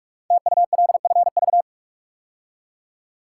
Morse code, heard as a keyed tone, about 44 wpm, 700 Hz. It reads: TULFV
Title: Code Groups - 5 Letters - 45wpm